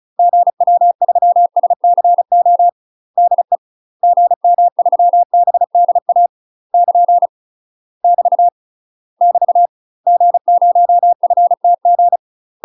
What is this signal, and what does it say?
Morse code, heard as a keyed tone, about 34 wpm, 700 Hz. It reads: GW3SCO DE GM3BDA <KN> <BT> <BT> G0FTG
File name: UK MASTER QSOs TRAINING/35WPM